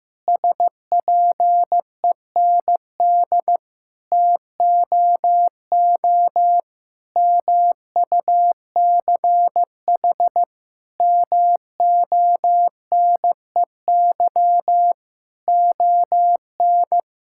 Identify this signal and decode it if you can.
Morse code, heard as a keyed tone, about 16 wpm, 700 Hz. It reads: SPEND TOO MUCH MONEY ON
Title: Sentences from Top 1000 Words - Review Part 5 - 15wpm